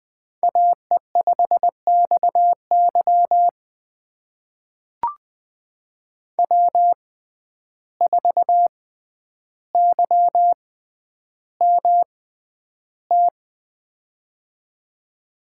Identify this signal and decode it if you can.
Morse code, heard as a keyed tone, about 20 wpm, spaced as for 10 wpm, 700 Hz. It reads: AE5XY W4YMT
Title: USA Call Signs - 5 Characters - 6x Character Spacing - 20wpm